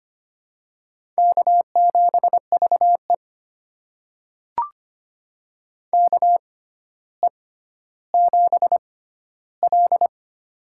Morse code, heard as a keyed tone, about 25 wpm, spaced as for 13 wpm, 700 Hz. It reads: K7VE KE7L
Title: USA Call Signs - 4 Characters - 6x Character Spacing - 25wpm